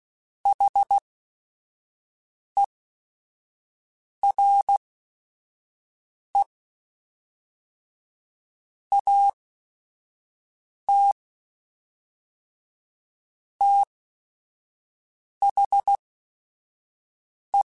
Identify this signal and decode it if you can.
Morse code, heard as a keyed tone, about 16 wpm, spaced as for 5 wpm, 800 Hz. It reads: HERE AT THE
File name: QSOs / QSOs-mp3s / WD0BGP 1@5